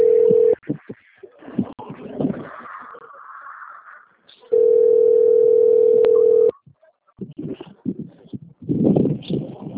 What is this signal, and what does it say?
Ringback tone (North America), 440 + 480 Hz.
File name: out- - - - - .54594